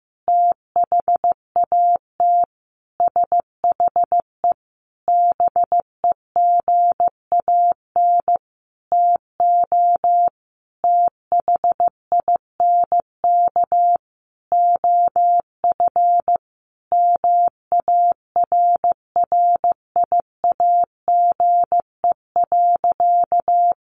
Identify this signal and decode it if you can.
Morse code, heard as a keyed tone, about 15 wpm, 700 Hz. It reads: THAT SHE BEGAN TO THINK OF MARRIAGE.
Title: Sentences from Top 800 Words 15wpm